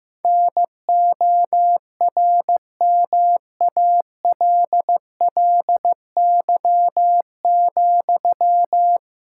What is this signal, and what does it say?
Morse code, heard as a keyed tone, about 15 wpm, 700 Hz. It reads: NORMALLY,
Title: Sentences from Top 1000 Words - Review Part 6 - 15wpm